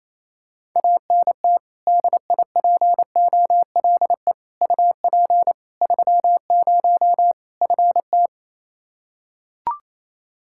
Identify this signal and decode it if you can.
Morse code, heard as a keyed tone, about 27 wpm, 700 Hz. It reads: ANT DIPOLE UP 30 FT
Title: K1USN SST QSO Practice 28wpm